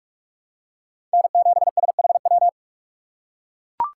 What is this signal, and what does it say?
Morse code, heard as a keyed tone, about 44 wpm, 700 Hz. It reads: N7SHW